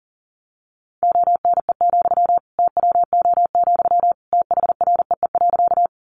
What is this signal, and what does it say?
Morse code, heard as a keyed tone, about 40 wpm, 700 Hz. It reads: ONE, TWO, THREE.
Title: Sentences from Top 200 Words 40wpm